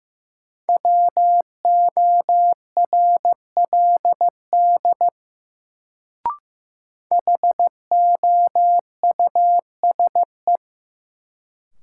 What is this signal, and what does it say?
Morse code, heard as a keyed tone, about 15 wpm, 700 Hz. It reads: WORLD HOUSE
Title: Sentences from Top 200 Words - New Word Drill - 15wpm